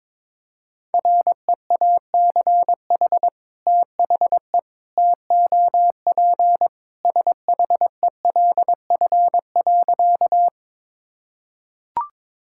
Morse code, heard as a keyed tone, about 21 wpm, 700 Hz. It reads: REACH THE TOP SHELF.
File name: practice / 22wpm / Sentences with 2-Word Phrases - 22wpm